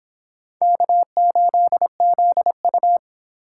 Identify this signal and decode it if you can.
Morse code, heard as a keyed tone, about 30 wpm, 700 Hz. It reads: K8ZU